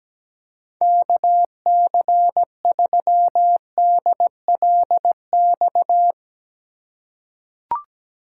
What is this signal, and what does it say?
Morse code, heard as a keyed tone, about 17 wpm, 700 Hz. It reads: KC3DLX
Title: USA Call Signs - 6 Characters - 3x Character Spacing - 17wpm